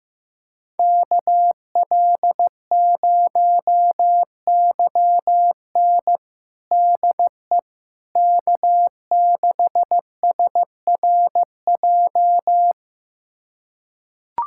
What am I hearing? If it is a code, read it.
Morse code, heard as a keyed tone, about 15 wpm, 700 Hz. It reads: KL0YN DE K6SRJ
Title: USA Callsign DE Callsign - 5 Characters - 15wpm